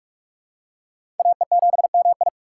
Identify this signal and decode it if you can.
Morse code, heard as a keyed tone, about 42 wpm, 700 Hz. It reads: AE7MI
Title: USA Call Signs - 5 Characters - 6x Character Spacing - 45wpm